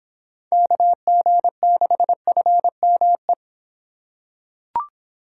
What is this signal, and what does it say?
Morse code, heard as a keyed tone, about 25 wpm, 700 Hz. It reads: KG6FME